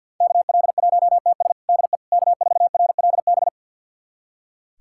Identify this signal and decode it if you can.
Morse code, heard as a keyed tone, about 46 wpm, 700 Hz. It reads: KL1TS DE K4RLB
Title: USA Callsign DE Callsign 50wpm